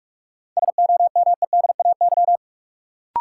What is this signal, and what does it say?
Morse code, heard as a keyed tone, about 47 wpm, 700 Hz. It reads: SOMEDAY
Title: Sentences from Top 1600 Words - New Word Drill - 45wpm